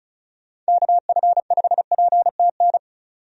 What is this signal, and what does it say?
Morse code, heard as a keyed tone, about 36 wpm, 700 Hz. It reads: KF5PTN
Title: USA Call Signs - 6 Characters - 35wpm